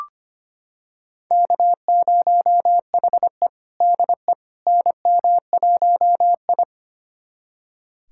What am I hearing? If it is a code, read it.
Morse code, heard as a keyed tone, about 25 wpm, 700 Hz. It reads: K0HE DE NM1I
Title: USA Callsign DE Callsign - 4 Characters - 25wpm